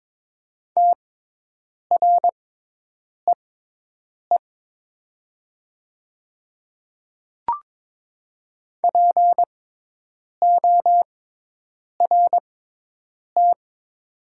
Morse code, heard as a keyed tone, about 22 wpm, spaced as for 8 wpm, 700 Hz. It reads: TREE PORT
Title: 6x Character Spacing - Top 300 Words - 22wpm